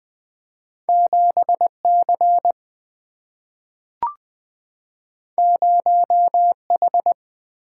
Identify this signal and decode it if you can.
Morse code, heard as a keyed tone, about 20 wpm, 690 Hz. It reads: 7C 0H
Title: Two Letter-Number Combinations - 20wpm